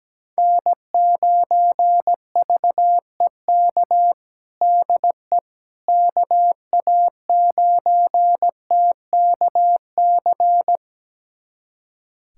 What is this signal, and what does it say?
Morse code, heard as a keyed tone, about 17 wpm, 700 Hz. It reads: N9VEK DE KA9TKC
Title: USA Callsign DE Callsign 17wpm